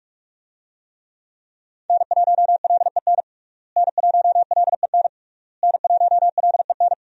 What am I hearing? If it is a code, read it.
Morse code, heard as a keyed tone, about 46 wpm, 700 Hz. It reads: N1LEN N1LEN N1LEN
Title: USA Call Signs - 5 Characters - Repeated 3x - 45wpm